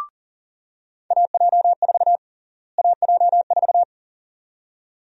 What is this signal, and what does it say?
Morse code, heard as a keyed tone, about 40 wpm, 700 Hz. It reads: AJ4 AJ4
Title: USA Call Sign Prefixes - Repeated 2x - 40wpm